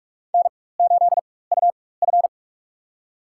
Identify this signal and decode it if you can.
Morse code, heard as a keyed tone, about 41 wpm, spaced as for 20 wpm, 700 Hz. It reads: N8UF
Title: USA Call Signs - 4 Characters - 4x Character Spacing - 45wpm